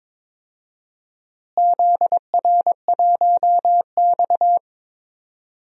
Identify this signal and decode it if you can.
Morse code, heard as a keyed tone, about 22 wpm, 700 Hz. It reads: ZR1X
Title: DXCC-100 Call Signs - 4 Characters - Repeated 3x - 22wpm